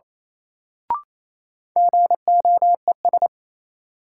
Morse code, heard as a keyed tone, about 28 wpm, 700 Hz. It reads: GOES
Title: Sentences from Top 1000 Words - Review Part 5 - New Word Drill - 28wpm